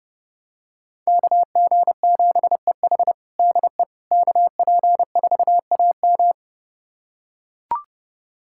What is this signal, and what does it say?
Morse code, heard as a keyed tone, about 30 wpm, 700 Hz. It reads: KG7EH DE KP4AM
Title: USA Callsign DE Callsign - 5 Characters - 30wpm